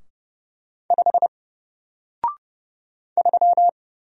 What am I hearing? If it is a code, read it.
Morse code, heard as a keyed tone, about 30 wpm, 700 Hz. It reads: 5 3
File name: Single-Letter-Number-30wpm